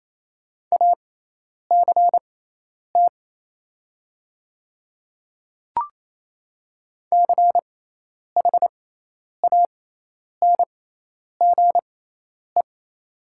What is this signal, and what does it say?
Morse code, heard as a keyed tone, about 27 wpm, spaced as for 10 wpm, 700 Hz. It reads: ACT CHANGE
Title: 6x Character Spacing - Top 200 Words - 28wpm